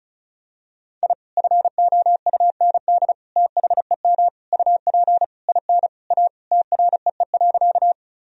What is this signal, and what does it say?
Morse code, heard as a keyed tone, about 34 wpm, 700 Hz. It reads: I FOUND THEM UP IN A TREE.
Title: Sentences from Top 1000 Words - Review Part 1 - 35wpm